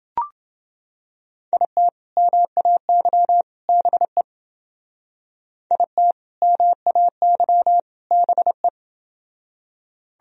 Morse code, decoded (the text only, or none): IT MAY BE IT MAY BE